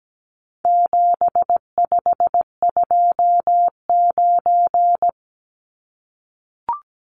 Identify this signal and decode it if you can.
Morse code, heard as a keyed tone, about 17 wpm, 700 Hz. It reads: 7529
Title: Numbers - 4-Digit 17wpm